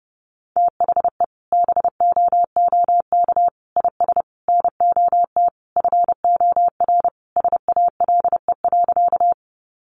Morse code, heard as a keyed tone, about 30 wpm, 700 Hz. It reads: THE BOOK IS NOT FOR SALE.
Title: Sentences from Top 1400 Words 30wpm